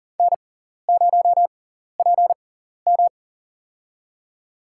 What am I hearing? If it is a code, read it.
Morse code, heard as a keyed tone, about 40 wpm, spaced as for 14 wpm, 700 Hz. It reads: N0PM
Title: USA Call Signs - 4 Characters - 6x Character Spacing - 40wpm